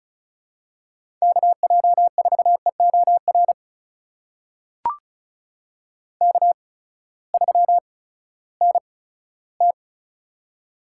Morse code, heard as a keyed tone, about 36 wpm, spaced as for 18 wpm, 700 Hz. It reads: KJ4EOR K3NT
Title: USA Call Signs - 8x Character Spacing - 35wpm